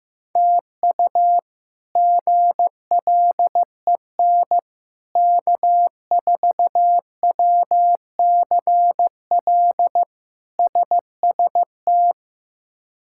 Morse code, heard as a keyed tone, about 15 wpm, 700 Hz. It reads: TU GLEN K4WCL SST
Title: K1USN SST QSO Practice - 15wpm